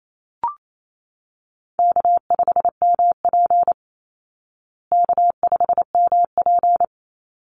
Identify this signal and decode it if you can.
Morse code, heard as a keyed tone, about 28 wpm, 700 Hz. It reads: K5MP K5MP
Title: Easy Call Signs Repeated 2x - 28wpm